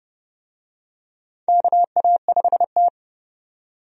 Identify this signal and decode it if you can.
Morse code, heard as a keyed tone, about 30 wpm, 700 Hz. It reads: KA5T
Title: USA Call Signs - 4 Characters - 30wpm